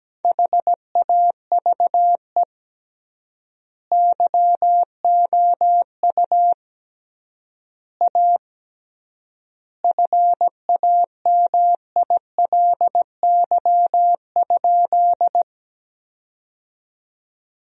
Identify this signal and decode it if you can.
Morse code, heard as a keyed tone, about 17 wpm, 700 Hz. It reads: HAVE YOU A FAMILY?
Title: 3x Word-Spacing - Sentences from Top 400 Words - 17wpm